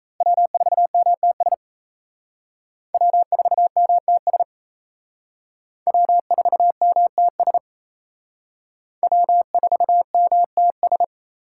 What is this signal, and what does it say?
Morse code, heard as a keyed tone, about 34 wpm, 700 Hz. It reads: W4MTS W4MTS W4MTS W4MTS
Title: Speed Racing - USA 1x3 Call Signs 28wpm